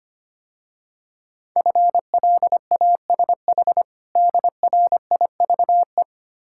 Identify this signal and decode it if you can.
Morse code, heard as a keyed tone, about 25 wpm, 700 Hz. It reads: FLASH DRIVE
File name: practice / 25wpm / Difficult Words - Speed-Racing with Hints - 25wpm